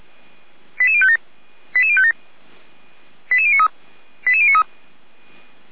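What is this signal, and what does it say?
Five-tone selective call (ZVEI), heard as quick runs of five tones, tones of about 70 ms. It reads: 88057 88057 88093 88093
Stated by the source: ZVEI five-tone selective call received by SDR, from the feuerwehr-orlamuende.org fire brigade recording server.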